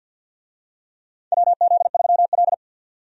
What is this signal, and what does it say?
Morse code, heard as a keyed tone, about 51 wpm, 700 Hz. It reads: WZ3L
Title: USA Call Signs - 4 Characters - 50wpm